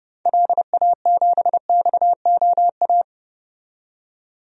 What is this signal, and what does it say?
Morse code, heard as a keyed tone, about 30 wpm, 700 Hz. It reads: LA7XOA
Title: DXCC-100 Call Signs - 6 Characters - 30wpm